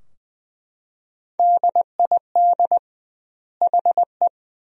Morse code, heard as a keyed tone, about 21 wpm, 700 Hz. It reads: DID HE
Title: 2x Word-Spacing - Sentences from Top 500 Words - 20wpm